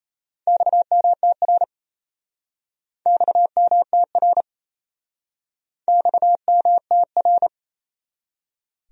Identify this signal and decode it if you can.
Morse code, heard as a keyed tone, about 32 wpm, 700 Hz. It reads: XMTR XMTR XMTR
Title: Speed-Racing Abbreviations - 28wpm